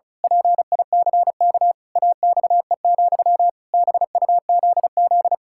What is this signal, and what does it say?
Morse code, heard as a keyed tone, about 36 wpm, 700 Hz. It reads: PICK AXE, BUZZ
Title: Pangram Challenge v2 - 35wpm